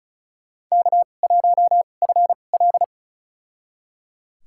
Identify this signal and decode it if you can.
Morse code, heard as a keyed tone, about 34 wpm, spaced as for 25 wpm, 700 Hz. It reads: K1FL
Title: USA Call Signs - 2x Character Spacing - 35wpm